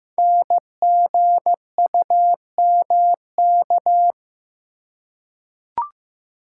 Morse code, heard as a keyed tone, about 15 wpm, 700 Hz. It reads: NGUMK
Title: Code Groups - 5 Letters - 15wpm